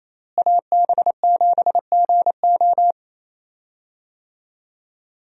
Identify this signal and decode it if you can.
Morse code, heard as a keyed tone, about 28 wpm, 700 Hz. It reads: AB7GO
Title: USA Call Signs - 5 Characters - Repeated 2x - 28wpm